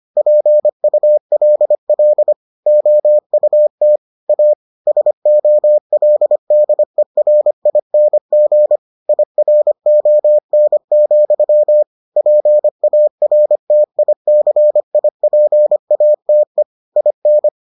Morse code, heard as a keyed tone, about 25 wpm, 600 Hz. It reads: PULL OUT A SOLDERING IRON, PARTICIPATE IN